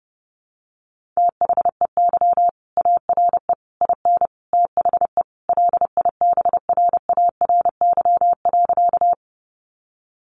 Morse code, heard as a keyed tone, about 28 wpm, 700 Hz. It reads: THEY ARE IN THE LIBRARY.
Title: Sentences from Top 600 Words 30wpm